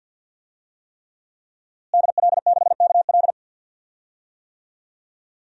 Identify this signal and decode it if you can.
Morse code, heard as a keyed tone, about 51 wpm, 700 Hz. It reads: DL6KL
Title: DXCC-100 Call Signs - 5 Characters - Repeated 3x - 50wpm